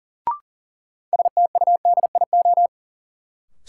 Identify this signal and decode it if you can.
Morse code, heard as a keyed tone, about 40 wpm, 700 Hz. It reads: STUDIO
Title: Sentences from Top 1200 Words - New Word Drill - 40wpm